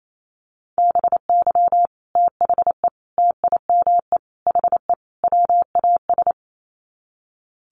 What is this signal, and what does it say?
Morse code, heard as a keyed tone, about 28 wpm, 700 Hz. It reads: BY THE TIME HE WAS
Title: Sets of 5 Words Encore 28wpm